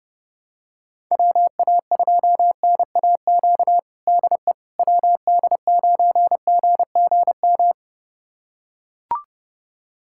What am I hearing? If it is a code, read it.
Morse code, heard as a keyed tone, about 30 wpm, 700 Hz. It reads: WA2NAQ DE WD9GGM
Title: USA Callsign DE Callsign - 6 Characters - 30wpm